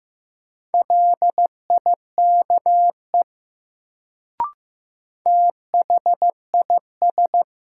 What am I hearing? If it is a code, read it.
Morse code, heard as a keyed tone, about 15 wpm, 700 Hz. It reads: LIKE THIS
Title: Sentences from Top 100 Words - New Word Drill - 15wpm